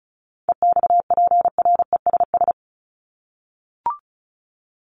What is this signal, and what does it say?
Morse code, heard as a keyed tone, about 34 wpm, 700 Hz. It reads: EXPRESS
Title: ED Suffix - Top 100 Words with Base Words 35wpm